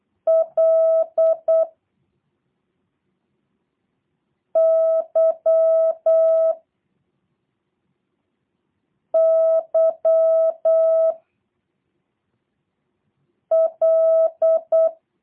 Morse code, heard as a keyed tone, about 8 wpm, 650 Hz. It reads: L Y Y L
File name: jh8cbh / 2023cw / review1-6